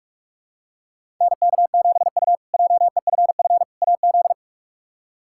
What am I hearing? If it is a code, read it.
Morse code, heard as a keyed tone, about 46 wpm, 700 Hz. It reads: NK7U JEFF AZ
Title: K1USN SST QSO Practice - 45wpm